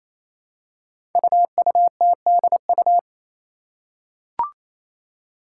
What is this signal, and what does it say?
Morse code, heard as a keyed tone, about 29 wpm, 700 Hz. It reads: UUTDU
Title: Code Groups - 5 Letters - 28wpm